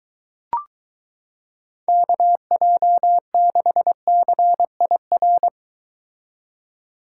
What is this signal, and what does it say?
Morse code, heard as a keyed tone, about 22 wpm, 700 Hz. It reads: KJ6CIR